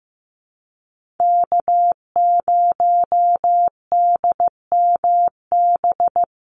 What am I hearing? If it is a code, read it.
Morse code, heard as a keyed tone, about 15 wpm, 700 Hz. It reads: K0DMB
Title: USA Call Signs Repeated 2x 15wpm